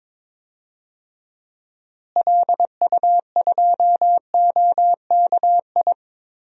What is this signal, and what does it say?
Morse code, heard as a keyed tone, about 21 wpm, 700 Hz. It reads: LU2OKI
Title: DXCC-100 Call Signs - 6 Characters - Repeated 3x - 22wpm